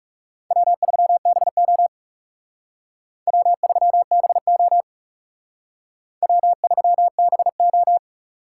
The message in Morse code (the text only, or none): W3BO W3BO W3BO